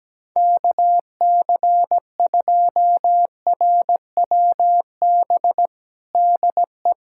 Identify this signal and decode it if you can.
Morse code, heard as a keyed tone, about 17 wpm, 700 Hz. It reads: KC2RWB DE
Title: USA Callsign DE Callsign 17wpm